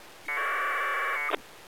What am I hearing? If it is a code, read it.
Packet radio (AX.25, as in APRS).